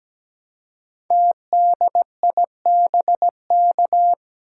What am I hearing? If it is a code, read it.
Morse code, heard as a keyed tone, about 17 wpm, 700 Hz. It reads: TDIBK
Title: Code Groups - 5 Letters - 17wpm